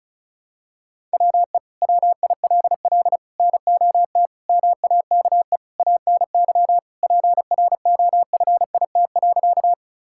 Morse code, heard as a keyed tone, about 34 wpm, 700 Hz. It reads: WE WILL NOT MAKE ANY PROFIT.
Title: Sentences from Top 1000 Words - Review Part 5 - 35wpm